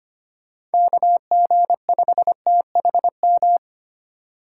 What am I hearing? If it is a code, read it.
Morse code, heard as a keyed tone, about 33 wpm, 700 Hz. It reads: KG5THM